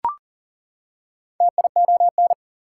Morse code, heard as a keyed tone, about 40 wpm, 700 Hz. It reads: TION
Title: Prefixes and Suffixes 40wpm